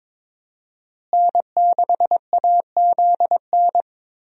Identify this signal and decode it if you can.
Morse code, heard as a keyed tone, about 22 wpm, 700 Hz. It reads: N6AZN
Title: USA Call Signs - 5 Characters - Repeated 2x - 22wpm